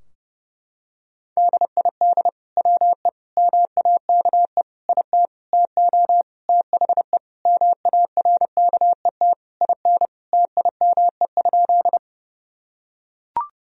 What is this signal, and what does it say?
Morse code, heard as a keyed tone, about 30 wpm, 700 Hz. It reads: DID WE MAKE IT TO THE MARKET IN TIME?
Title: Sentences from Top 1000 Words - Review Part 3 - 30wpm